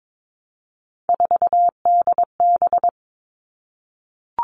Morse code, heard as a keyed tone, about 22 wpm, 700 Hz. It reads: 4DB
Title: All Cut Numbers - 3-Digit 22wpm